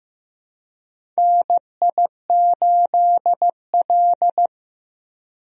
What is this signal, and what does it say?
Morse code, heard as a keyed tone, about 15 wpm, 700 Hz. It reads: NI8L
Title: USA Call Signs - 4 Characters - Repeated 3x - 15wpm